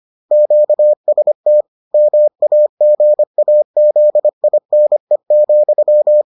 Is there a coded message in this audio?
Yes: Morse code from 0.3 to 6.2 s, heard as a keyed tone, about 25 wpm, 600 Hz. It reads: QST MAGAZINE,